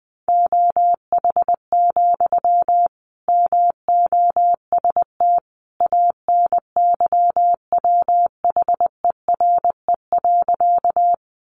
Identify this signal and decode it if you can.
Morse code, heard as a keyed tone, about 20 wpm, 700 Hz. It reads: OH, MOST ANYWHERE.
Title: Sentences from Top 1400 Words 20wpm